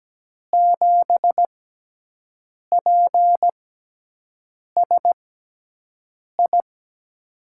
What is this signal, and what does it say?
Morse code, heard as a keyed tone, about 17 wpm, spaced as for 6 wpm, 700 Hz. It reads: 7PSI
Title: USA Call Signs - 6 Characters - 6x Character Spacing - 17wpm